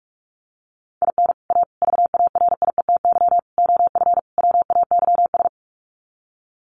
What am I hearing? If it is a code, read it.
Morse code, heard as a keyed tone, about 44 wpm, 700 Hz. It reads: IN A VARIETY OF WAYS
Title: Sets of 5 Words Encore 45wpm